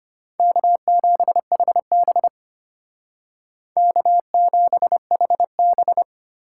Morse code, heard as a keyed tone, about 32 wpm, 700 Hz. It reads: K7HB K7HB